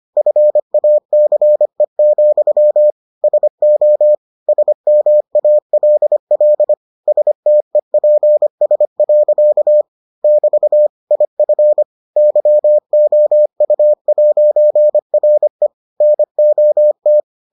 Morse code, heard as a keyed tone, about 25 wpm, 600 Hz. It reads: FACE, SO SMALL STEPS. <BT> IF YOU'RE NOT